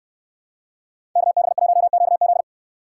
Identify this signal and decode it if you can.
Morse code, heard as a keyed tone, about 69 wpm, 700 Hz. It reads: KB1QZ